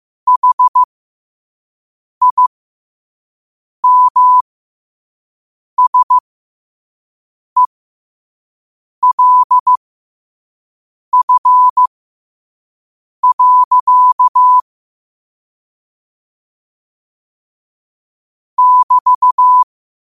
Morse code, heard as a keyed tone, about 15 wpm, spaced as for 5 wpm, 1000 Hz. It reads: HIMSELF. <BT>
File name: qotdfiles / 5wpm